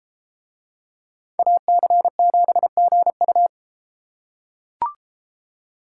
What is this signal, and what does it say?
Morse code, heard as a keyed tone, about 32 wpm, 700 Hz. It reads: AC7GU